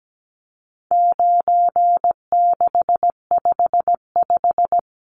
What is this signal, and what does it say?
Morse code, heard as a keyed tone, about 17 wpm, 700 Hz. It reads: 9655
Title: Numbers - 4-Digit 17wpm